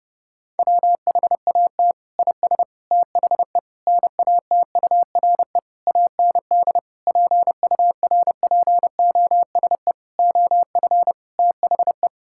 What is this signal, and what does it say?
Morse code, heard as a keyed tone, about 30 wpm, 700 Hz. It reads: WHAT IS THE NATURE AND PURPOSE OF THE
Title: Sentences from Top 1000 Words - Review Part 5 - 30wpm